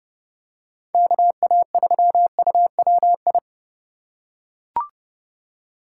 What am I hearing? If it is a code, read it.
Morse code, heard as a keyed tone, about 30 wpm, 700 Hz. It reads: KA3UWI